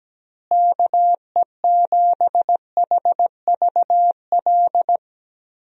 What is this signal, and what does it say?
Morse code, heard as a keyed tone, about 17 wpm, 700 Hz. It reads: KE7HVL
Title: USA Call Signs - 6 Characters - Repeated 2x - 17wpm